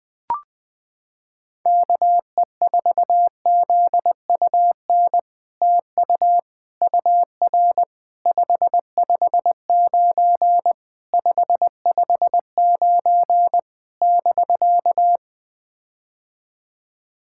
Morse code, heard as a keyed tone, about 20 wpm, 700 Hz. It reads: KE4ZUN TU UR 559 559 <BK>
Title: POTA QSO Practice - 20wpm